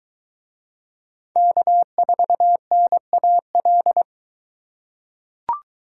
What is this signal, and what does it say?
Morse code, heard as a keyed tone, about 22 wpm, 700 Hz. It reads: K4NAL